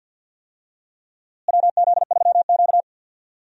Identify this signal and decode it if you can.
Morse code, heard as a keyed tone, about 51 wpm, 700 Hz. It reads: WZ3Q
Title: USA Call Signs - 4 Characters - Repeated 2x - 50wpm